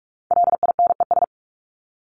Morse code, heard as a keyed tone, about 44 wpm, 700 Hz. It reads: LINES
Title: ES:S Suffix - Top 100 Words with Base Words 45wpm